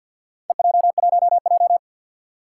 Morse code, heard as a keyed tone, about 50 wpm, 700 Hz. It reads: EJ1J
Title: Four Challenging Letter-Number Combinations 50wpm